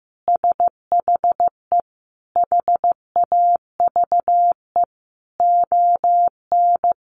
Morse code, heard as a keyed tone, about 15 wpm, 700 Hz. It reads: SHE HAVE ON
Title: Sentences from Top 1200 Words 15wpm